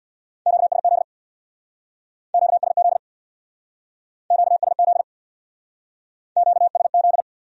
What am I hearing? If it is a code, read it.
Morse code, heard as a keyed tone, about 63 wpm, 700 Hz. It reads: QSZ QSZ QSZ QSZ